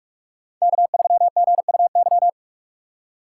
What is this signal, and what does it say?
Morse code, heard as a keyed tone, about 44 wpm, 700 Hz. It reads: K3GUY
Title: USA Call Signs - 6x Character Spacing - 45wpm